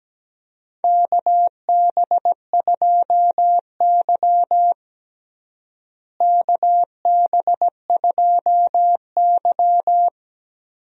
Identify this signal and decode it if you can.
Morse code, heard as a keyed tone, about 17 wpm, 700 Hz. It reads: KB2Y KB2Y
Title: USA Call Signs - 4 Characters - Repeated 3x - 17wpm